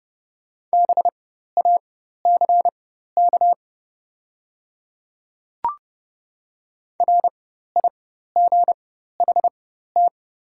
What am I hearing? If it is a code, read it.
Morse code, heard as a keyed tone, about 30 wpm, spaced as for 14 wpm, 700 Hz. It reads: BACK RIGHT
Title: 4x Character Spacing - Top 200 Words - 30wpm